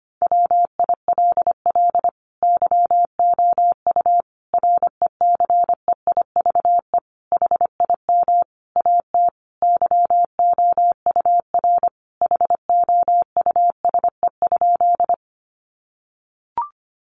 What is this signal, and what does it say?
Morse code, heard as a keyed tone, about 25 wpm, 700 Hz. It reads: WILL YOU RECEIVE HIM AT YOUR HOUSE?
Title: Sentences from Top 1000 Words 25wpm